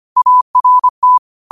Morse code, heard as a keyed tone, about 25 wpm, 1000 Hz. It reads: ART
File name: qotdfiles / 25wpm